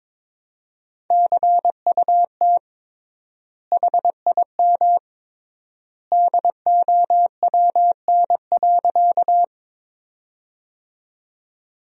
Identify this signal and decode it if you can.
Morse code, heard as a keyed tone, about 22 wpm, 700 Hz. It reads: CUT HIM DOWN.
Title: 3x Word-Spacing - Sentences from Top 300 Words - 22wpm